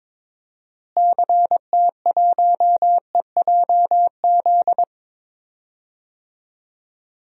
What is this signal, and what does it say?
Morse code, heard as a keyed tone, about 21 wpm, 700 Hz. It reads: CT1EJZ
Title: DXCC-100 Call Signs - 6 Characters - Repeated 2x - 22wpm